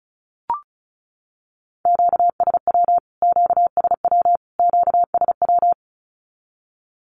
Morse code, heard as a keyed tone, about 34 wpm, 700 Hz. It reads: QSW QSW QSW
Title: Q-Codes Repeated 3x 35wpm